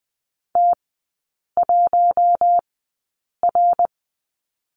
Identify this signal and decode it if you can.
Morse code, heard as a keyed tone, about 20 wpm, spaced as for 8 wpm, 700 Hz. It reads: T1R
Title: 060 Lesson 11e - Letter h - Three Characters 20wpm